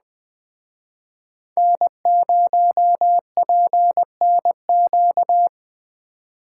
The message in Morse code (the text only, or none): N0PNQ